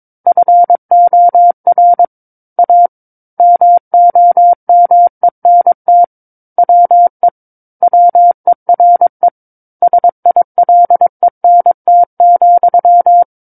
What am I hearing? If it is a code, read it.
Morse code, heard as a keyed tone, about 22 wpm, 700 Hz. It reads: FOR A MOMENT WE WERE SILENT,
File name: Book One / 13-Chapter 13 - 22 WPM